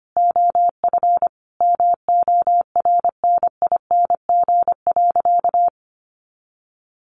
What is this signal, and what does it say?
Morse code, heard as a keyed tone, about 25 wpm, 700 Hz. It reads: OF MORNING.
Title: A Sound Of Thunder 25wpm